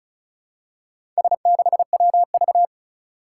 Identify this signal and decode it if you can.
Morse code, heard as a keyed tone, about 34 wpm, 700 Hz. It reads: S6WV
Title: Four Challenging Letter-Number Combinations 35wpm